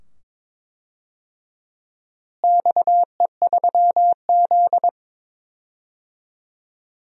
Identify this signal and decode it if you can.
Morse code, heard as a keyed tone, about 22 wpm, 700 Hz. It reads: XE3Z
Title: DXCC-100 Call Signs - 4 Characters - Repeated 2x - 22wpm